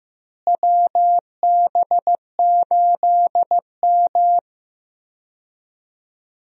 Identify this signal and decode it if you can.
Morse code, heard as a keyed tone, about 16 wpm, 700 Hz. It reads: WB8M
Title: USA Call Signs - 4 Characters - Repeated 3x - 15wpm